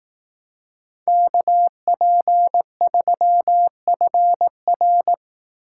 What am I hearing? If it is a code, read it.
Morse code, heard as a keyed tone, about 18 wpm, 700 Hz. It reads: KP3FR
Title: Speed Racing - USA 2x2 Call Signs 15wpm